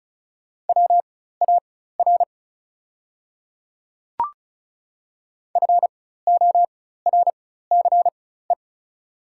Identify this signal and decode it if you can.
Morse code, heard as a keyed tone, about 36 wpm, spaced as for 16 wpm, 690 Hz. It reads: WAR FORCE
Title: 4x Character Spacing - Top 500 Words - 35wpm